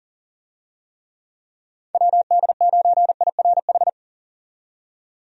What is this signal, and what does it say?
Morse code, heard as a keyed tone, about 40 wpm, 700 Hz. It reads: WD9IRH
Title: USA Call Signs - 6 Characters - 40wpm